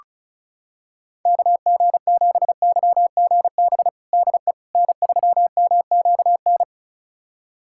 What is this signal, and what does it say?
Morse code, heard as a keyed tone, about 34 wpm, 700 Hz. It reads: KG7YGB DE N3MQN
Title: USA Callsign DE Callsign 35wpm